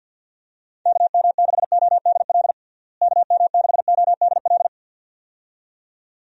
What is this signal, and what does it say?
Morse code, heard as a keyed tone, about 46 wpm, 700 Hz. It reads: KM6ODL KM6ODL
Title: USA Call Signs - 6 Characters - Repeated 2x - 50wpm